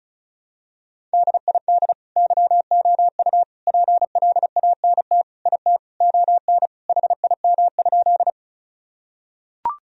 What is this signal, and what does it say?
Morse code, heard as a keyed tone, about 36 wpm, 700 Hz. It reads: DID YOU PLANT IT ON HIM?
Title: Sentences from Top 1000 Words - Review Part 1 - 35wpm